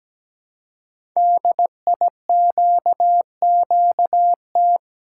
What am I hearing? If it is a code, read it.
Morse code, heard as a keyed tone, about 17 wpm, 700 Hz. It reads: DIQQT
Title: Code Groups - 5 Letters - 17wpm